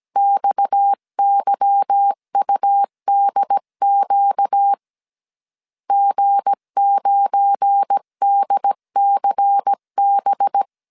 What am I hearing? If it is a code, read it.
Morse code, heard as a keyed tone, about 17 wpm, 800 Hz. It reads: XYUDQ G9DCB
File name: CW / DET / 1 9 1500C S B 16wpm 2856